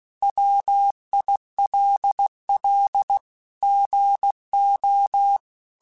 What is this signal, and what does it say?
Morse code, heard as a keyed tone, about 16 wpm, 800 Hz. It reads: WILL GO
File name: Technical Subjects / Technical Subjects-mp3s / OR Gate @15